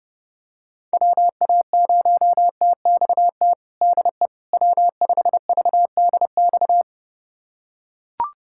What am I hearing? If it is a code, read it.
Morse code, heard as a keyed tone, about 32 wpm, 700 Hz. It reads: WA0TXT DE W5VDX
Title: USA Callsign DE Callsign 30wpm